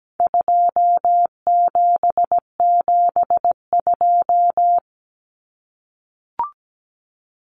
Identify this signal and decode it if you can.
Morse code, heard as a keyed tone, about 17 wpm, 700 Hz. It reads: 2772
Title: Common Cut Numbers - 4-Digit 17wpm